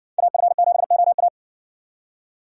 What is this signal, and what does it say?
Morse code, heard as a keyed tone, about 60 wpm, 700 Hz. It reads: AF7OA